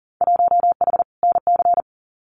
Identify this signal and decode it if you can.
Morse code, heard as a keyed tone, about 42 wpm, 700 Hz. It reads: 1H NC
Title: Winter Field Day Exchanges 40wpm